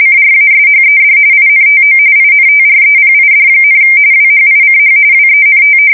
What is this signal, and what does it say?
RTTY, about 45.45 baud. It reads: CRASHED INTO INTREPID, WHICH WENT
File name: fsk170 / battleshipnj